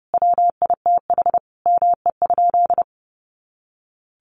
Morse code, heard as a keyed tone, about 30 wpm, 700 Hz. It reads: WITH ME?
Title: Sentences from Top 200 Words 30wpm